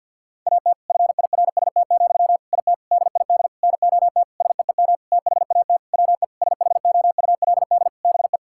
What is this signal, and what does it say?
Morse code, heard as a keyed tone, about 51 wpm, 700 Hz. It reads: AT FIRST, IT DID NOT SEEM THAT WE SHOULD BE
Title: Sentences from Top 1000 Words - Review Part 6 - 50wpm